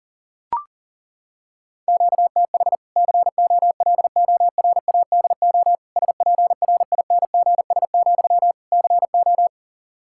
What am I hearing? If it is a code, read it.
Morse code, heard as a keyed tone, about 40 wpm, 700 Hz. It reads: QTH COLORADO SPRINGS, CO
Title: Practice QSO - Exchange 1 - Beginning - 40wpm